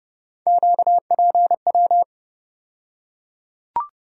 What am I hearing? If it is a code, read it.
Morse code, heard as a keyed tone, about 30 wpm, 700 Hz. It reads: QPW
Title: USA Call Sign Postfixes - 3 Letters 30wpm